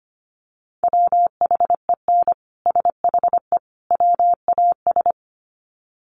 Morse code, heard as a keyed tone, about 25 wpm, 700 Hz. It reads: WHEN SHE WAS
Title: Sets of 3 Words 25wpm